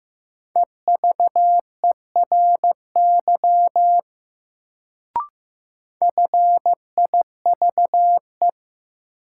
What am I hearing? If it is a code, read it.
Morse code, heard as a keyed tone, about 15 wpm, 700 Hz. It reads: EVERY FIVE
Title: Sets of 5 Words - New Word Drill - 15wpm